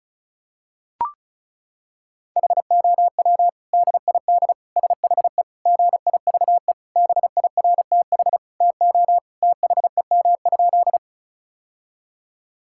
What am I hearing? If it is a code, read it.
Morse code, heard as a keyed tone, about 36 wpm, 700 Hz. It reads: HOW DID SHE GIVE BIRTH TO THEM?
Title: Sentences from Top 1000 Words - Review Part 5 - 35wpm